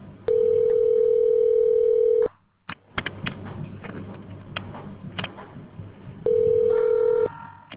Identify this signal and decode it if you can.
Ringback tone (North America), 440 + 480 Hz.